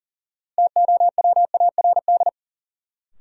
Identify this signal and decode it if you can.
Morse code, heard as a keyed tone, about 37 wpm, 700 Hz. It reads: TOWARD
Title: Sentences from Top 1000 Words - Review Part 6 - New Word Drill - 40wpm